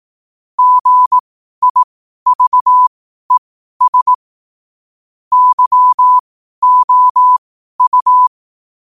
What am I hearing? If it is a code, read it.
Morse code, heard as a keyed tone, about 18 wpm, spaced as for 12 wpm, 1000 Hz. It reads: GIVES YOU
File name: qotdfiles / 12wpm